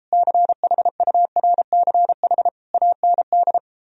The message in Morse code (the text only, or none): CHURCH AND